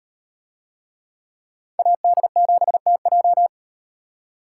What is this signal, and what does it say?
Morse code, heard as a keyed tone, about 38 wpm, 700 Hz. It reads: AD7TJ